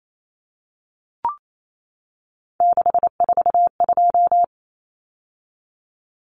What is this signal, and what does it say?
Morse code, heard as a keyed tone, about 28 wpm, 700 Hz. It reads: 642
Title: Numbers - 3-Digit 28wpm